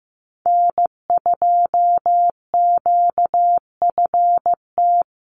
Morse code, heard as a keyed tone, about 15 wpm, 700 Hz. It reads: N2QFT
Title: USA Call Signs 15wpm